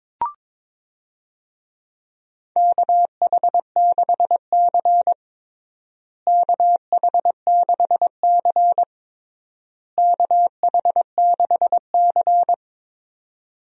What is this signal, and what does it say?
Morse code, heard as a keyed tone, about 22 wpm, 700 Hz. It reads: KH6C KH6C KH6C
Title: DXCC-100 Call Signs - 4 Characters - Repeated 3x - 22wpm